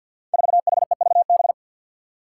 Morse code, heard as a keyed tone, about 47 wpm, 700 Hz. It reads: 4HEVB
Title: Five Challenging Letter-Number Combinations 50wpm